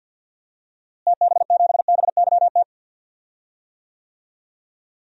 Morse code, heard as a keyed tone, about 51 wpm, 700 Hz. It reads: TB7BYT
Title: DXCC-100 Call Signs - 6 Characters - Repeated 3x - 50wpm